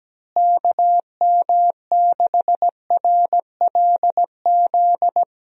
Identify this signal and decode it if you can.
Morse code, heard as a keyed tone, about 17 wpm, 700 Hz. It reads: KM6RLZ
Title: USA Call Signs - 6 Characters - Repeated 3x - 17wpm